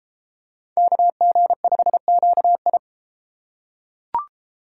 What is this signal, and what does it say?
Morse code, heard as a keyed tone, about 30 wpm, 700 Hz. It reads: KG5QI